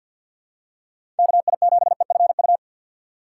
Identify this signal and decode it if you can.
Morse code, heard as a keyed tone, about 53 wpm, 700 Hz. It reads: KI7EFU
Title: USA Call Signs - 2x Character Spacing - 50wpm